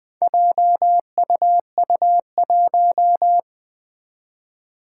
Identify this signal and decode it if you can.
Morse code, heard as a keyed tone, about 20 wpm, 700 Hz. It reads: JUU1
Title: Four Challenging Letter-Number Combinations 20wpm